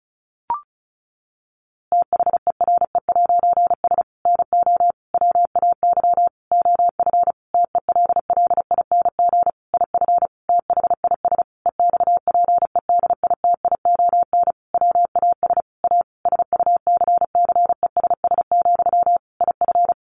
Morse code, heard as a keyed tone, about 34 wpm, 700 Hz. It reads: THERE'S NO WAY OF TELLING IF THIS EXPEDITION WAS A SUCCESS, IF
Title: A Sound Of Thunder 35wpm